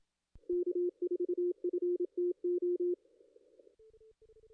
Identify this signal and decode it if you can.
Morse code, heard as a keyed tone, about 27 wpm, 350 Hz. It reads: K4FTO
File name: contests / ss-cw-k5tr-op-n5rz -2300